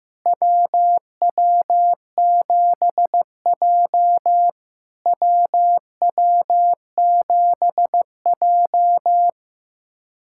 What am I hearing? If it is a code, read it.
Morse code, heard as a keyed tone, about 15 wpm, 700 Hz. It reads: WW7J WW7J
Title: SOTA QSO Practice 15wpm